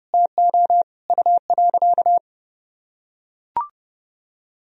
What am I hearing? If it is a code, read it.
Morse code, heard as a keyed tone, about 32 wpm, 700 Hz. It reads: TO U.
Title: Sentences from Top 500 Words and Abbreviations 30wpm